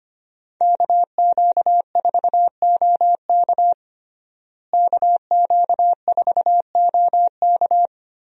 Morse code, heard as a keyed tone, about 25 wpm, 700 Hz. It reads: KQ4OK KQ4OK
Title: USA Call Signs - 5 Characters - Repeated 3x - 25wpm